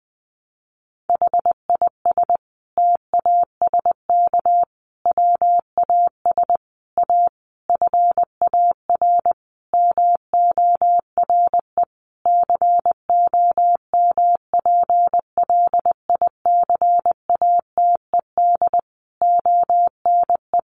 Morse code, heard as a keyed tone, about 20 wpm, 700 Hz. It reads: HIS TASK WAS A FAR MORE COMPLICATED ONE
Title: Sentences from Top 1200 Words 20wpm